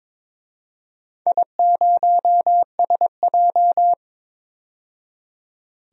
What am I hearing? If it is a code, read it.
Morse code, heard as a keyed tone, about 22 wpm, 700 Hz. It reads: I0SJ
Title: DXCC-100 Call Signs - 4 Characters - 22wpm